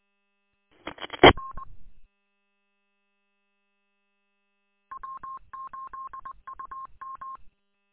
Morse code, heard as a keyed tone, about 20 wpm, 1070 Hz. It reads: R W8UM
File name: w8um- -combined